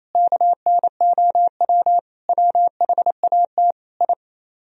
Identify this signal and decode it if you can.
Morse code, heard as a keyed tone, about 37 wpm, 700 Hz. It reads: KNOW WHAT I